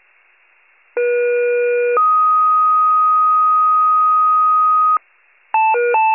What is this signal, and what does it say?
Two-tone page, heard as a one-second tone, then a three-second tone, about 480 then 1250 Hz. It is heard from 1.0 to 5.0 s.
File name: wcalert